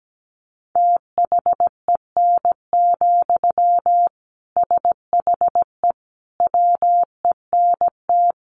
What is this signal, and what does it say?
Morse code, heard as a keyed tone, about 17 wpm, 700 Hz. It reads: THEN, SHE WENT
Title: Sentences from Top 1500 Words 17wpm